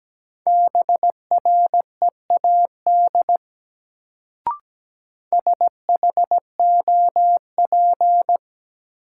Morse code, heard as a keyed tone, about 17 wpm, 700 Hz. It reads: BREAD SHOP
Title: Sentences from Top 1600 Words - New Word Drill Alt - 17wpm